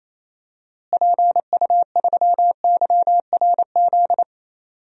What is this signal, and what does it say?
Morse code, heard as a keyed tone, about 28 wpm, 700 Hz. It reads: PU3YRZ
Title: DXCC-100 Call Signs - 6 Characters - Repeated 3x - 28wpm